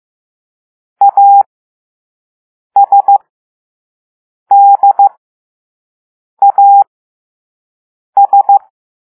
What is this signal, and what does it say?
Morse code, heard as a keyed tone, about 15 wpm, spaced as for 5 wpm, 800 Hz. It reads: ASDAS
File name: Morse / Level3B